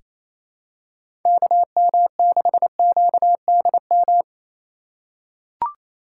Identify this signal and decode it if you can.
Morse code, heard as a keyed tone, about 27 wpm, 700 Hz. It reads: KM6QDM
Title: USA Call Signs - 4x Character Spacing - 28wpm